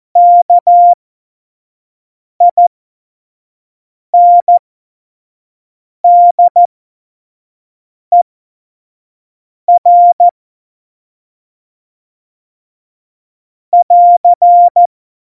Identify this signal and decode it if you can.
Morse code, heard as a keyed tone, about 14 wpm, spaced as for 5 wpm, 700 Hz. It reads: KINDER <AR>
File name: Morsekurs / Witze / Witz 19. Januar 2021, 25 F70